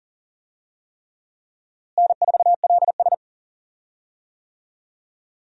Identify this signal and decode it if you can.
Morse code, heard as a keyed tone, about 40 wpm, 700 Hz. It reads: N4LS
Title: USA Call Signs - 4 Characters - Repeated 3x - 40wpm